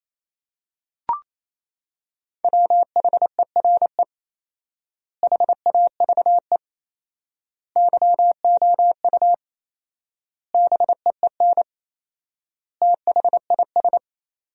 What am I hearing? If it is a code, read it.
Morse code, heard as a keyed tone, about 27 wpm, 700 Hz. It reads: WHERE HAVE YOU BEEN THIS
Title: 4x Word-Spacing - Sentences from Top 300 Words - 28wpm